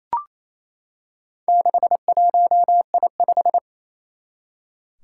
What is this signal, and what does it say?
Morse code, heard as a keyed tone, about 27 wpm, 700 Hz. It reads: 61I5
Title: Four Challenging Letter-Number Combinations 28wpm